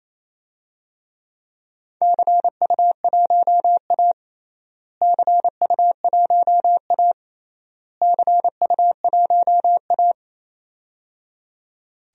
Morse code, heard as a keyed tone, about 28 wpm, 700 Hz. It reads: CU1A CU1A CU1A
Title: DXCC-100 Call Signs - 4 Characters - Repeated 3x - 28wpm